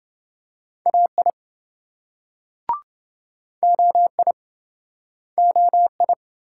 Morse code, heard as a keyed tone, about 27 wpm, 700 Hz. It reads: AI OI OI